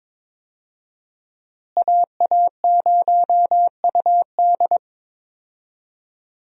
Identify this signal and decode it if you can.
Morse code, heard as a keyed tone, about 22 wpm, 700 Hz. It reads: AA0UD
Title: USA Call Signs - 5 Characters - Repeated 3x - 22wpm